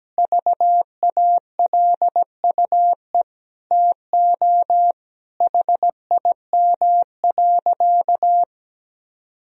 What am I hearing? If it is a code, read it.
Morse code, heard as a keyed tone, about 17 wpm, 700 Hz. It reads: VALUE TO HIM.
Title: Sentences from Top 1000 Words - Review Part 3 - 17wpm